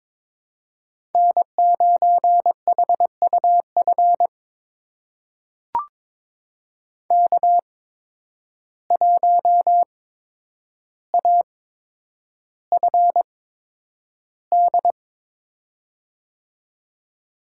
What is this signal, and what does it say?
Morse code, heard as a keyed tone, about 22 wpm, spaced as for 9 wpm, 700 Hz. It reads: N9HUF K1AFD
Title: USA Call Signs - 5 Characters - 8x Character Spacing - 22wpm